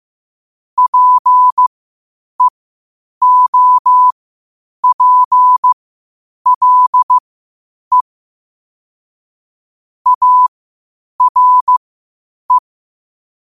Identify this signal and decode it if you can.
Morse code, heard as a keyed tone, about 15 wpm, spaced as for 8 wpm, 1000 Hz. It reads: PEOPLE ARE
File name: qotdfiles / 8wpm